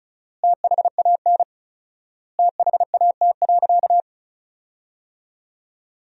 Morse code, heard as a keyed tone, about 36 wpm, 700 Hz. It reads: THAN THAT.
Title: 4x Word-Spacing - Sentences from Top 400 Words - 35wpm